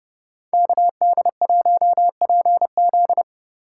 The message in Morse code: KD1PZ

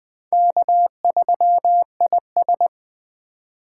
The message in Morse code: K3IS

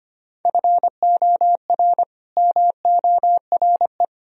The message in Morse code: FOR MORE

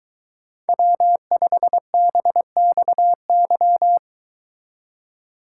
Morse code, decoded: W5BXY

